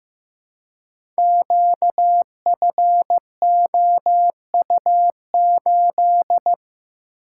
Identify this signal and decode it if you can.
Morse code, heard as a keyed tone, about 15 wpm, 700 Hz. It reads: QFOU8
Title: Code Groups - 5 Letter:Numbers - 15wpm